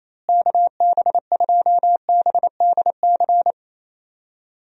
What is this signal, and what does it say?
Morse code, heard as a keyed tone, about 28 wpm, 700 Hz. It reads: KB2BDC
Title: USA Call Signs - 6 Characters - Repeated 2x - 28wpm